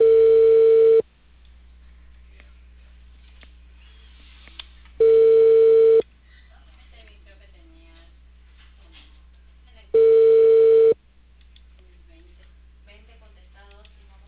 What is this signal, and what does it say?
Ringback tone (China, 450 Hz), with speech.